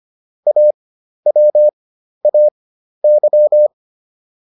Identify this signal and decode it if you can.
Morse code, heard as a keyed tone, about 25 wpm, spaced as for 12 wpm, 600 Hz. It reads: AWAY